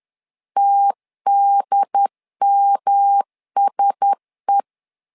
Morse code, heard as a keyed tone, about 10 wpm, 800 Hz. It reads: TDMSE
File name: LEZIONI / 4 Q1S3 10wpm 2427